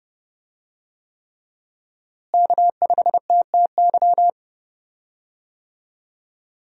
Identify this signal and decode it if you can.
Morse code, heard as a keyed tone, about 31 wpm, 700 Hz. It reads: K5TTY
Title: USA Call Signs - 5 Characters - Repeated 3x - 30wpm